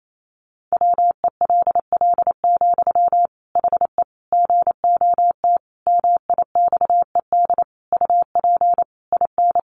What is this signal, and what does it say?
Morse code, heard as a keyed tone, about 28 wpm, 700 Hz. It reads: WELL, HE GOT MIXED UP IN
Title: Sentences from Top 1500 Words 28wpm